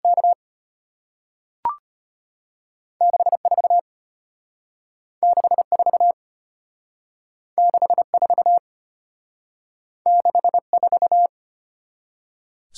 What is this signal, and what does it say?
Morse code, heard as a keyed tone, about 33 wpm, 700 Hz. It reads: K 64 64 64 64